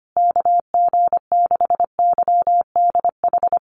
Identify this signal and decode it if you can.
Morse code, heard as a keyed tone, about 25 wpm, 700 Hz. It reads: KG6YDH
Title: USA Call Signs Repeated 3x 25wpm